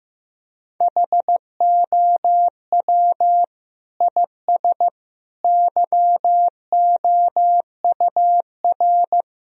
Morse code, heard as a keyed tone, about 16 wpm, 700 Hz. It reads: HOW IS YOUR
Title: Sentences from Top 1000 Words - Review Part 3 - 15wpm